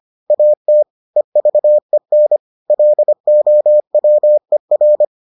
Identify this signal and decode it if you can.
Morse code, heard as a keyed tone, about 25 wpm, 600 Hz. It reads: AT EVEN LOWER